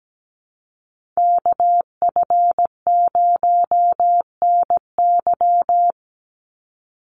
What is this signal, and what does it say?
Morse code, heard as a keyed tone, about 17 wpm, 700 Hz. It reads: KF0NY
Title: USA Call Signs 17wpm